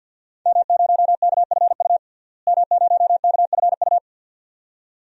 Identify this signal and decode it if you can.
Morse code, heard as a keyed tone, about 52 wpm, 700 Hz. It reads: M0KFU M0KFU
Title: DXCC-100 Call Signs - 5 Characters - Repeated 2x - 50wpm